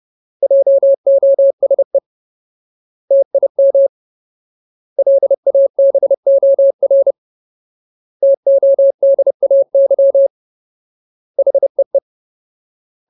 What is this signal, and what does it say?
Morse code, heard as a keyed tone, about 30 wpm, 550 Hz. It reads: JOSE TIM LABOR TODAY HEE
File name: Ultimate Word List / 30WPM Ultimate